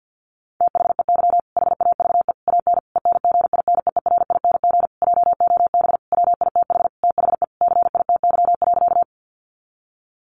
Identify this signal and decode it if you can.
Morse code, heard as a keyed tone, about 50 wpm, 700 Hz. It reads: THEY HAVE AN ENGINEERING JOB WITH THE CITY.
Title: Sentences from Top 1500 Words 50wpm